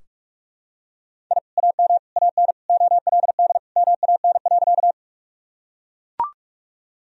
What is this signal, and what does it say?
Morse code, heard as a keyed tone, about 41 wpm, 700 Hz. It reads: I AM AN OLD MAN.
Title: Sentences from Top 1000 Words - Review Part 2 - 45wpm